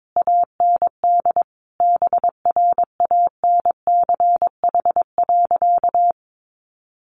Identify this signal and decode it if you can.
Morse code, heard as a keyed tone, about 22 wpm, 700 Hz. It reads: AND BRANCH.
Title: A Sound Of Thunder 22wpm